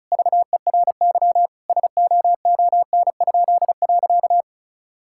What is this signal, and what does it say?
Morse code, heard as a keyed tone, about 36 wpm, 700 Hz. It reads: VERY SOON?.
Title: Pangram Challenge v2 - 35wpm